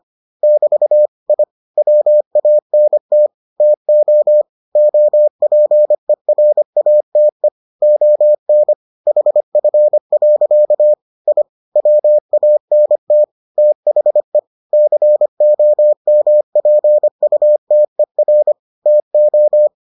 Morse code, heard as a keyed tone, about 25 wpm, 600 Hz. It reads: <BT> I WANT TO OPERATE ON HF. I WANT THE COMPUTER TO